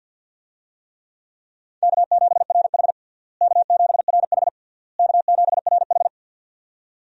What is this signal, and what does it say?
Morse code, heard as a keyed tone, about 51 wpm, 700 Hz. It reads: K7RH K7RH K7RH
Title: USA Call Signs - 4 Characters - Repeated 3x - 50wpm